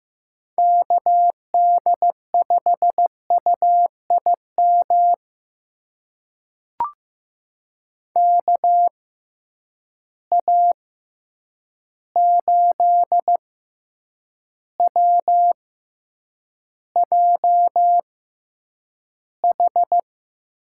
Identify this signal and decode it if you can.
Morse code, heard as a keyed tone, about 15 wpm, spaced as for 8 wpm, 700 Hz. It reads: KD5UIM KA8WJH
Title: USA Call Signs - 6 Characters - 6x Character Spacing - 15wpm